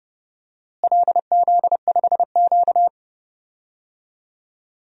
Morse code, heard as a keyed tone, about 30 wpm, 700 Hz. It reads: LZ5Q
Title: DXCC-100 Call Signs - 4 Characters - Repeated 2x - 30wpm